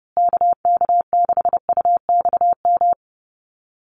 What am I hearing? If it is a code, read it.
Morse code, heard as a keyed tone, about 30 wpm, 700 Hz. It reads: KK6UXM
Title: USA Call Signs Repeated 2x 30wpm